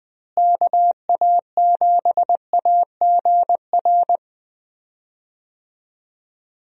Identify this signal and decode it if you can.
Morse code, heard as a keyed tone, about 20 wpm, 700 Hz. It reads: KA7AGR
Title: USA Call Signs - 6 Characters - Repeated 2x - 20wpm